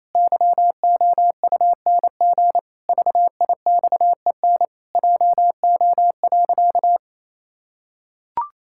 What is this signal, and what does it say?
Morse code, heard as a keyed tone, about 28 wpm, 700 Hz. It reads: YOUNG VIXEN JO.
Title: Pangram Challenge v2 - 28wpm